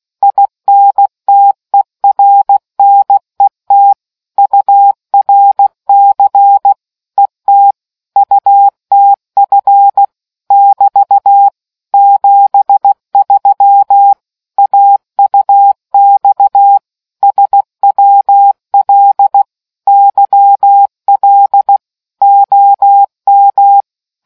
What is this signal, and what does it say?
Morse code, heard as a keyed tone, about 16 wpm, 800 Hz. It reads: INTERNET URC ET UTF <BT> 73 AUX SWL YL OM